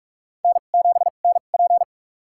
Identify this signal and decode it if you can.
Morse code, heard as a keyed tone, about 43 wpm, spaced as for 32 wpm, 700 Hz. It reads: N7NP
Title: USA Call Signs - 4 Characters - 2x Character Spacing - 45wpm